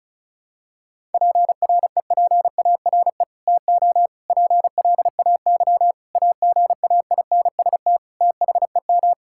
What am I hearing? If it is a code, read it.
Morse code, heard as a keyed tone, about 34 wpm, 700 Hz. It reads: PREPARE TO PLAY AGAINST THEM
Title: Sentences from Top 1000 Words - Review Part 6 - 35wpm